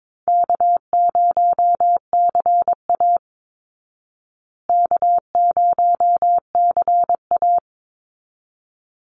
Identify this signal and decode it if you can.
Morse code, heard as a keyed tone, about 22 wpm, 690 Hz. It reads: K0CA K0CA
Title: Easy Call Signs Repeated 2x - 22wpm